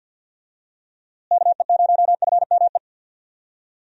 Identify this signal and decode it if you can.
Morse code, heard as a keyed tone, about 47 wpm, 700 Hz. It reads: KE0FME
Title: USA Call Signs - 6 Characters - Repeated 3x - 50wpm